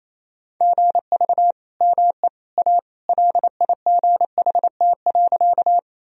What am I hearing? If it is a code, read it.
Morse code, heard as a keyed tone, about 28 wpm, 700 Hz. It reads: GV ME A LIGHT.
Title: Sentences from Top 200 Words and Abbreviations 28wpm